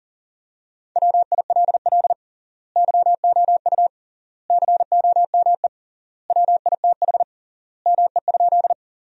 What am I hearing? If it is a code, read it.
Morse code, heard as a keyed tone, about 40 wpm, 700 Hz. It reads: WILL YOU COME WITH ME?
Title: 3x Word-Spacing - Sentences from Top 200 Words - 40wpm